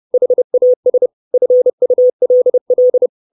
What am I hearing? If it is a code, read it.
Morse code, heard as a keyed tone, about 30 wpm, 500 Hz. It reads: HAS FULL
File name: 632 Words 30WPM